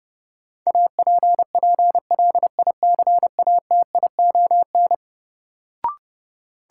Morse code, heard as a keyed tone, about 32 wpm, 700 Hz. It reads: APPLICATION
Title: Sentences from Top 1200 Words - New Word Drill - 30wpm